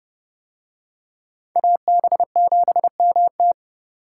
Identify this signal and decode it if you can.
Morse code, heard as a keyed tone, about 30 wpm, 700 Hz. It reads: AB7MT